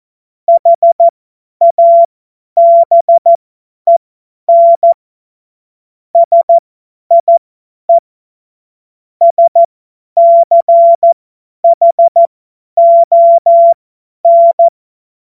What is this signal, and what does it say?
Morse code, heard as a keyed tone, about 14 wpm, spaced as for 10 wpm, 680 Hz. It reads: HABEN SIE SCHON
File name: Morsekurs / Witze / Witz 10.Oktober 2023 50 F70